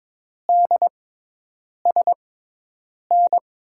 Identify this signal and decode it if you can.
Morse code, heard as a keyed tone, about 22 wpm, spaced as for 8 wpm, 700 Hz. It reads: DSN
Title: USA Call Signs - 5 Characters - 6x Character Spacing - 22wpm